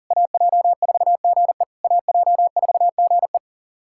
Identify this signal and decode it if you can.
Morse code, heard as a keyed tone, about 40 wpm, 700 Hz. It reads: AJ4GE AJ4GE
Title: SOTA QSO Practice 40wpm